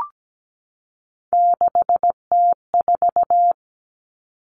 Morse code, heard as a keyed tone, about 17 wpm, 700 Hz. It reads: 6T4
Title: Common Cut Numbers - 3-Digit 17wpm